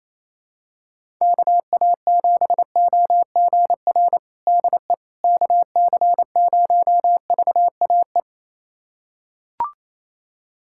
Morse code, heard as a keyed tone, about 28 wpm, 700 Hz. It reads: KA7OGR DE KC0VAE
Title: USA Callsign DE Callsign - 6 Characters - 28wpm